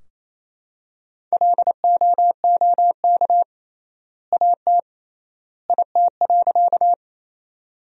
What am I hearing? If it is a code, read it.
Morse code, heard as a keyed tone, about 28 wpm, 700 Hz. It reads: LOOK AT IT.
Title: 3x Word-Spacing - Sentences from Top 100 Words - 28wpm